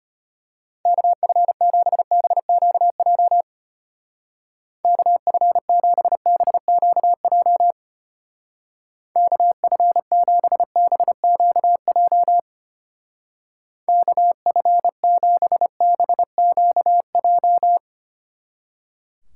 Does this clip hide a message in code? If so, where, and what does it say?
Yes: Morse code from 0.9 to 17.8 s, heard as a keyed tone, about 32 wpm, 700 Hz. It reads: KF7BQJ KF7BQJ KF7BQJ KF7BQJ